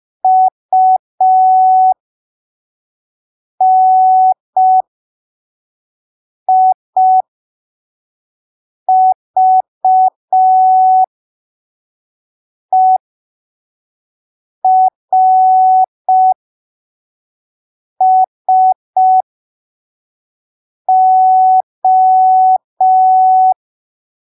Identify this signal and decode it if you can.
Morse code, heard as a keyed tone, about 5 wpm, spaced as for 3 wpm, 750 Hz. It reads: UNIVERSO
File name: arquivos / texto / texto 17 05ppm